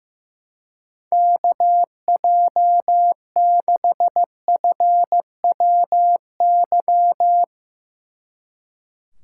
Morse code, heard as a keyed tone, about 15 wpm, 700 Hz. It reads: KJ6FWY
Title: Speed Racing - USA Call Signs 15wpm